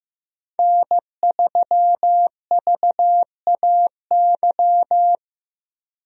Morse code, heard as a keyed tone, about 15 wpm, 700 Hz. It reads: N3VAY
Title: USA Call Signs - 5 Characters - Repeated 2x - 15wpm